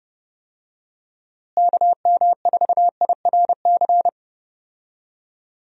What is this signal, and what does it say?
Morse code, heard as a keyed tone, about 30 wpm, 700 Hz. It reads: KM4IRC
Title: USA Call Signs - 6 Characters - Repeated 3x - 30wpm